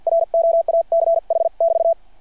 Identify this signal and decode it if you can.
Morse code, heard as a keyed tone, about 52 wpm, spaced as for 43 wpm, 660 Hz. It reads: ROACH<BT>
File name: archived / CRSnet